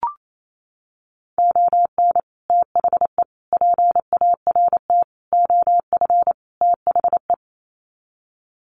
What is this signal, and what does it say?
Morse code, heard as a keyed tone, about 28 wpm, 700 Hz. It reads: ON THE PART OF THE
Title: Sets of 5 Words 28wpm